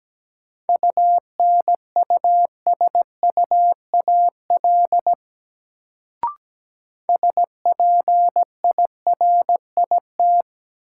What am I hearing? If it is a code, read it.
Morse code, heard as a keyed tone, about 17 wpm, 700 Hz. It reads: UNUSUAL SPIRIT
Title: Sentences from Top 1400 Words - New Word Drill - 17wpm